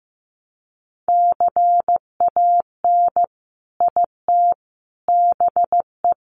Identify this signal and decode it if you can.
Morse code, heard as a keyed tone, about 15 wpm, 700 Hz. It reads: CAN IT BE
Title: Sentences from Top 500 Words 15wpm